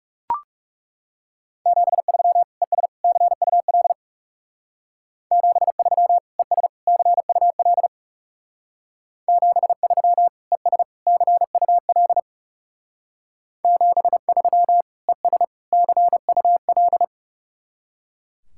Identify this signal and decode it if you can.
Morse code, heard as a keyed tone, about 37 wpm, 700 Hz. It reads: 73 ES CUL 73 ES CUL 73 ES CUL 73 ES CUL
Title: Speed-Racing Common QSO Phrases Encore - 30wpm